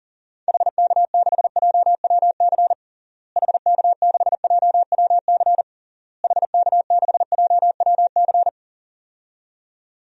Morse code, heard as a keyed tone, about 40 wpm, 700 Hz. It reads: HK6JWC HK6JWC HK6JWC
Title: DXCC-100 Call Signs - 6 Characters - Repeated 3x - 40wpm